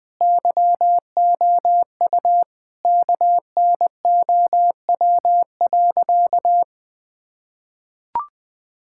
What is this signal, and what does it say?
Morse code, heard as a keyed tone, about 20 wpm, 700 Hz. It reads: YOU KNOW.
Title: Sentences from Top 1000 Words - Review Part 5 - 20wpm